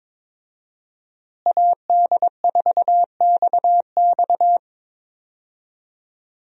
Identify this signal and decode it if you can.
Morse code, heard as a keyed tone, about 21 wpm, 700 Hz. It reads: AD4XX
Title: USA Call Signs - 5 Characters - Repeated 3x - 22wpm